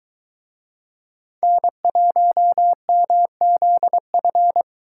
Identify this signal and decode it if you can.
Morse code, heard as a keyed tone, about 22 wpm, 700 Hz. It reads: N1MZF